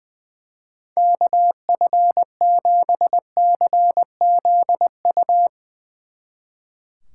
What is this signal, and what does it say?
Morse code, heard as a keyed tone, about 21 wpm, 700 Hz. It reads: KF7CZU
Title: Speed Racing - USA 2x3 Call Signs 20wpm